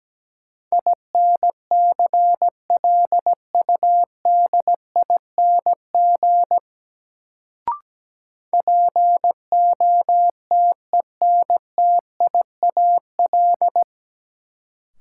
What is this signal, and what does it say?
Morse code, heard as a keyed tone, about 17 wpm, 700 Hz. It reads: INCLUDING POTENTIAL
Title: Sentences from Top 1000 Words - Review Part 4 - New Word Drill - 17wpm